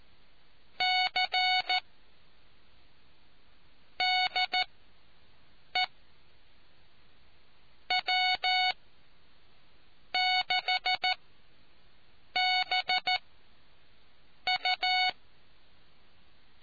Morse code, heard as a keyed tone, about 14 wpm, spaced as for 6 wpm, 740 Hz. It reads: C DE W6BU